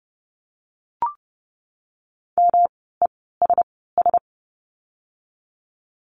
Morse code, heard as a keyed tone, about 30 wpm, spaced as for 17 wpm, 700 Hz. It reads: MESS
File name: course / 30:10wpm / 108 Lesson 18f - Letter m - Words 30wpm